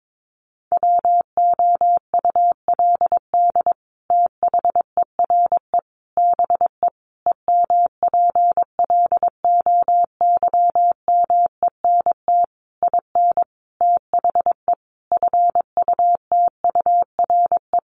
Morse code, heard as a keyed tone, about 22 wpm, 700 Hz. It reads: WOULD THERE BE EMPLOYMENT IN THE FUTURE
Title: Sentences from Top 1500 Words 22wpm